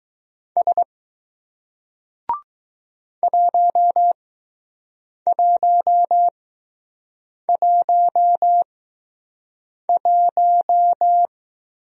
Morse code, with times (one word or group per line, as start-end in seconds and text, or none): 0.6-0.8 S
3.2-11.3 1111